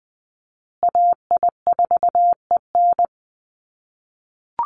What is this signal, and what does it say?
Morse code, heard as a keyed tone, about 20 wpm, 700 Hz. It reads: AI4EN
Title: USA Call Signs Repeated 2x 20wpm